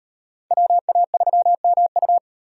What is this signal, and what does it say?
Morse code, heard as a keyed tone, about 35 wpm, 700 Hz. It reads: WA3MU